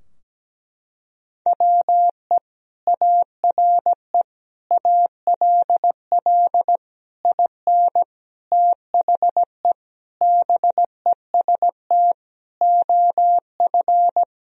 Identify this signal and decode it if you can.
Morse code, heard as a keyed tone, about 17 wpm, 700 Hz. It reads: WE ARE ALL IN THE BEST OF
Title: Sentences from Top 1000 Words - Review Part 3 - 17wpm